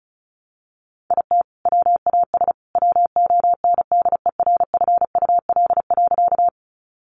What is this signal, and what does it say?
Morse code, heard as a keyed tone, about 34 wpm, 700 Hz. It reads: IT WAS WONDERFUL.
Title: Sentences from Top 1100 Words 35wpm